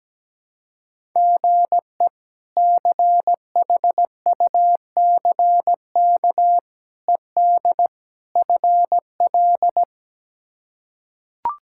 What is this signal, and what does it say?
Morse code, heard as a keyed tone, about 17 wpm, 700 Hz. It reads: GE CHUCK ED FL
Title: K1USN SST QSO Practice - 17wpm